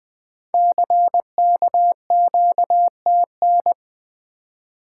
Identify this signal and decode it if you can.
Morse code, heard as a keyed tone, about 20 wpm, 700 Hz. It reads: CKQTN
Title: Code Groups - 5 Letters - 20wpm